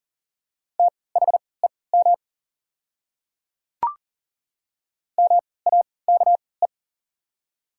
Morse code, heard as a keyed tone, about 37 wpm, spaced as for 22 wpm, 690 Hz. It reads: THEM MAKE
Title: 3x Character Spacing - Top 300 Words - 40wpm